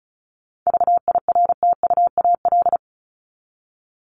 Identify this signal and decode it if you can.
Morse code, heard as a keyed tone, about 34 wpm, 700 Hz. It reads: VIRTUAL
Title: LY Suffix - Top 100 Words with Base Words 35wpm